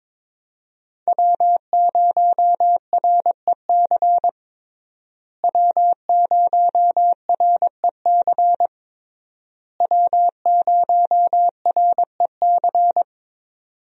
Morse code, heard as a keyed tone, about 22 wpm, 700 Hz. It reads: W0REC W0REC W0REC
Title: USA Call Signs - 5 Characters - Repeated 3x - 22wpm